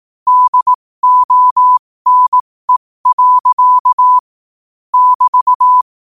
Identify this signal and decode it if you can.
Morse code, heard as a keyed tone, about 18 wpm, spaced as for 15 wpm, 1000 Hz. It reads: DONE. <BT>
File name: qotdfiles / 15wpm